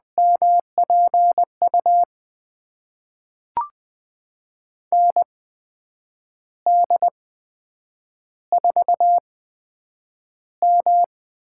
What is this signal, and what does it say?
Morse code, heard as a keyed tone, about 20 wpm, spaced as for 8 wpm, 700 Hz. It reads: MPU ND4M
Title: USA Call Signs - 8x Character Spacing - 20wpm